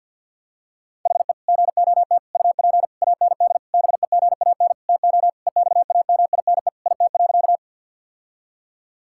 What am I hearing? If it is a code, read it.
Morse code, heard as a keyed tone, about 50 wpm, 700 Hz. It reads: HE GOT UP AND BEGAN TO EXAMINE IT.
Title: Sentences from Top 1600 Words 50wpm